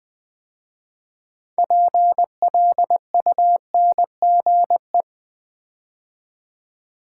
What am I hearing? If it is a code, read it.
Morse code, heard as a keyed tone, about 20 wpm, 700 Hz. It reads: PLUNGE
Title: 6 Letter Words - First 3 Letters Given - 20wpm